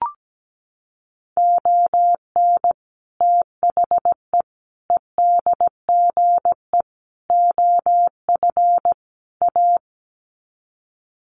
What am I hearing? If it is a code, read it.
Morse code, heard as a keyed tone, about 17 wpm, 700 Hz. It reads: ON THE EDGE OF A
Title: Sets of 5 Words Encore2 17wpm